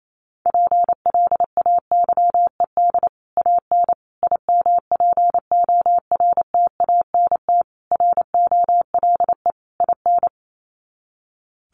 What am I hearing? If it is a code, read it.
Morse code, heard as a keyed tone, about 28 wpm, 700 Hz. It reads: PLAYED AN IMPORTANT ROLE IN
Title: Sets of 5 Words Encore2 28wpm